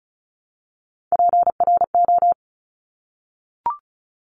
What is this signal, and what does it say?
Morse code, heard as a keyed tone, about 34 wpm, 700 Hz. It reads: PRO
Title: Top Three Letter Combinations in Words 35wpm